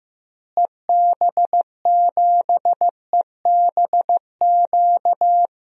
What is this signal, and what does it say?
Morse code, heard as a keyed tone, about 15 wpm, 700 Hz. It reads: EB7EBQ
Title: DXCC-100 Call Signs - 6 Characters - Repeated 3x - 15wpm